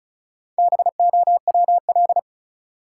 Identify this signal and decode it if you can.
Morse code, heard as a keyed tone, about 34 wpm, 700 Hz. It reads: BOWL
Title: Sentences from Top 1200 Words - New Word Drill - 35wpm